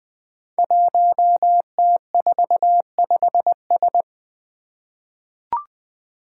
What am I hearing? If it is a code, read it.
Morse code, heard as a keyed tone, about 20 wpm, 700 Hz. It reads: 1T45S
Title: Five Challenging Letter-Number Combinations 20wpm